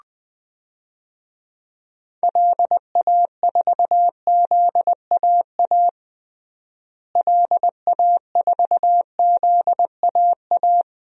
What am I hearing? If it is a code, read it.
Morse code, heard as a keyed tone, about 20 wpm, 700 Hz. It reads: LA4ZAA LA4ZAA
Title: DXCC-100 Call Signs - 6 Characters - Repeated 2x - 20wpm